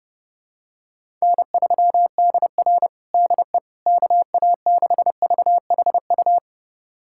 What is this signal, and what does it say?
Morse code, heard as a keyed tone, about 30 wpm, 700 Hz. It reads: N3DR DE KA6VHU
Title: USA Callsign DE Callsign 30wpm